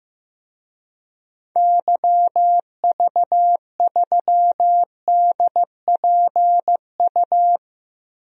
Morse code, heard as a keyed tone, about 15 wpm, 700 Hz. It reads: YV3DPU
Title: DXCC-100 Call Signs - 6 Characters - 15wpm